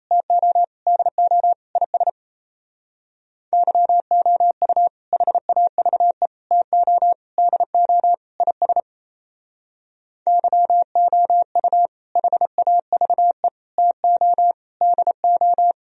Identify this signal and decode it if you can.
Morse code, heard as a keyed tone, about 34 wpm, 700 Hz. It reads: TO DO IS YOU HAVE TO DO IS YOU HAVE TO DO
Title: Speed-Racing - Sets of 5 Words 28wpm